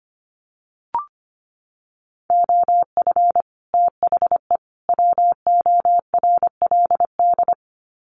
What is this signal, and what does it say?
Morse code, heard as a keyed tone, about 25 wpm, 700 Hz. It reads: OF THE WORLD
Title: Sets of 3 Words 25wpm